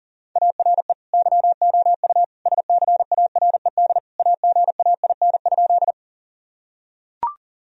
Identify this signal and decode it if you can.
Morse code, heard as a keyed tone, about 37 wpm, 700 Hz. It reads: ARE YOU SCARED AGAIN?
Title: Sentences from Top 1000 Words - Review Part 4 - 40wpm